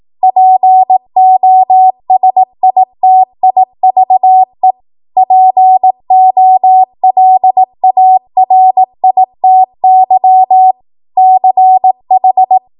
Morse code, heard as a keyed tone, about 18 wpm, 750 Hz. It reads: POSITIVE POLARITY CH
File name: Morse / 18WPM